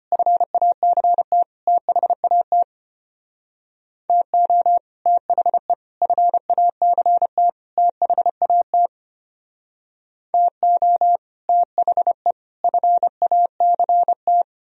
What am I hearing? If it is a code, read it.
Morse code, heard as a keyed tone, about 32 wpm, 700 Hz. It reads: FACT THAT TO THE FACT THAT TO THE FACT